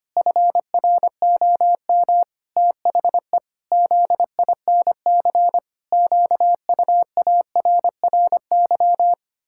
Morse code, heard as a keyed tone, about 25 wpm, 700 Hz. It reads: FROM THE ZINC QUARRY
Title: Pangram Challenge v2 - 25wpm